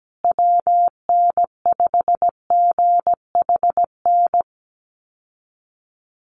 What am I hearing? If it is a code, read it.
Morse code, heard as a keyed tone, about 17 wpm, 700 Hz. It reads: WN5GHN
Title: USA Call Signs Repeated 2x 17wpm